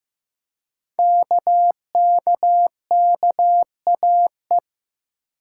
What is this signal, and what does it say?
Morse code, heard as a keyed tone, about 15 wpm, 700 Hz. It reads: KKKAE
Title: Code Groups - 5 Letters - 15wpm